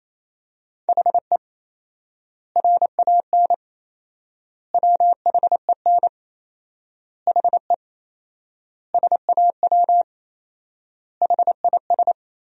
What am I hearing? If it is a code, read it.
Morse code, heard as a keyed tone, about 28 wpm, 700 Hz. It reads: HE RAN WHEN HE SAW HIS
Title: 4x Word-Spacing - Sentences from Top 500 Words - 28wpm